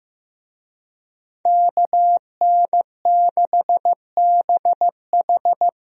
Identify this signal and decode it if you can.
Morse code, heard as a keyed tone, about 15 wpm, 700 Hz. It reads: KN6BH
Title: USA Call Signs - 5 Characters - Repeated 2x - 15wpm